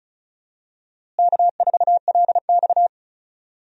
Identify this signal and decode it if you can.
Morse code, heard as a keyed tone, about 34 wpm, 700 Hz. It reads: K4LX
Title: USA Call Signs - 4 Characters - Repeated 3x - 35wpm